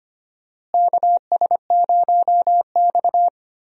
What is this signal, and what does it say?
Morse code, heard as a keyed tone, about 25 wpm, 700 Hz. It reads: KS0X
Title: USA Call Signs - 4 Characters - Repeated 2x - 25wpm